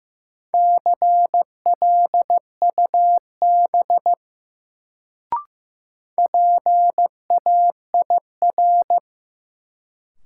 Morse code, heard as a keyed tone, about 15 wpm, 700 Hz. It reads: CLUB PAIR
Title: Sentences from Top 1200 Words - New Word Drill - 15wpm